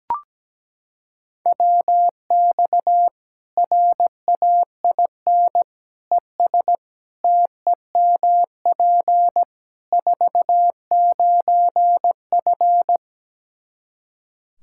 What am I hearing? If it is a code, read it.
Morse code, heard as a keyed tone, about 17 wpm, 700 Hz. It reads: WX RAIN ES TEMP 49F
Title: Practice QSO - Exchange 2 - Info - 17wpm